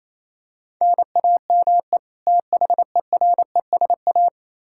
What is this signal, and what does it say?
Morse code, heard as a keyed tone, about 28 wpm, 700 Hz. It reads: NAME THERESA
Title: K1USN SST QSO Practice 28wpm